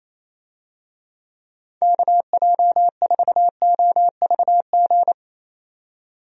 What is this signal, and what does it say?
Morse code, heard as a keyed tone, about 27 wpm, 700 Hz. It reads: KJ4OVG
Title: USA Call Signs - 6 Characters - 28wpm